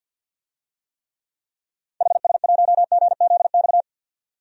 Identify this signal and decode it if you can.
Morse code, heard as a keyed tone, about 46 wpm, 700 Hz. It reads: HS1GZX
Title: DXCC-100 Call Signs - 6 Characters - 50wpm